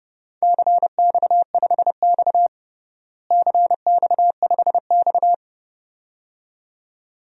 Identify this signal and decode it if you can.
Morse code, heard as a keyed tone, about 32 wpm, 700 Hz. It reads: CX5X CX5X
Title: DXCC-100 Call Signs - 4 Characters - Repeated 2x - 30wpm